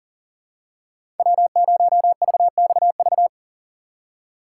Morse code, heard as a keyed tone, about 38 wpm, 700 Hz. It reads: W0VXV
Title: USA Call Signs - 5 Characters - 8x Character Spacing - 40wpm